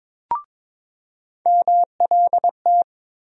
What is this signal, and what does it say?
Morse code, heard as a keyed tone, about 21 wpm, 700 Hz. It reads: MLT
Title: USA Call Sign Postfixes - 3 Letters 22wpm